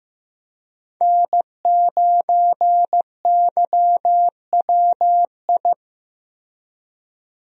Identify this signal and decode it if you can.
Morse code, heard as a keyed tone, about 15 wpm, 700 Hz. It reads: N9YWI
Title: USA Call Signs - 5 Characters - Repeated 3x - 15wpm